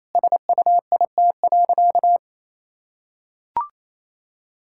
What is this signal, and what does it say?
Morse code, heard as a keyed tone, about 28 wpm, 700 Hz. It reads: SUIT.
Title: Sentences from Top 1600 Words 28wpm